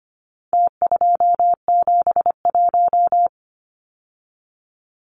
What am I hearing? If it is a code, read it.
Morse code, heard as a keyed tone, about 25 wpm, 700 Hz. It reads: T271
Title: Numbers - Contest Style - 4-Digit 25wpm